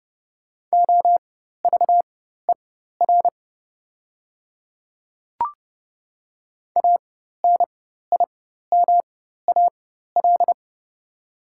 Morse code, heard as a keyed tone, about 30 wpm, spaced as for 14 wpm, 700 Hz. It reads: OVER ANIMAL
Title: 4x Character Spacing - Top 200 Words - 30wpm